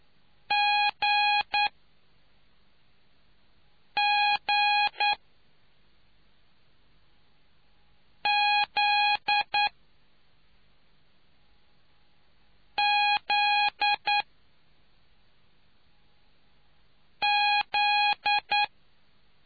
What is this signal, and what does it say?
Morse code, heard as a keyed tone, about 9 wpm, 820 Hz. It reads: G G Z Z Z